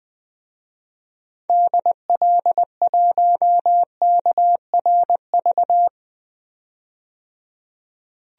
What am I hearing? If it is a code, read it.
Morse code, heard as a keyed tone, about 21 wpm, 700 Hz. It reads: DL1KRV
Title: DXCC-100 Call Signs - 6 Characters - 20wpm